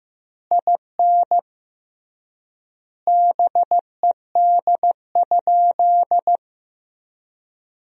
Morse code, heard as a keyed tone, about 15 wpm, 700 Hz. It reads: IN BED?
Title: 3x Word-Spacing - Sentences from Top 500 Words - 15wpm